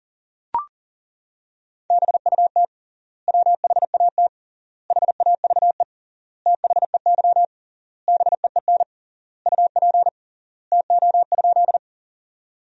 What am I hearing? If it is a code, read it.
Morse code, heard as a keyed tone, about 40 wpm, 700 Hz. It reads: BUT WHAT HAVE THEY BEEN UP TO?
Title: 3x Word-Spacing - Sentences from Top 100 Words - 40wpm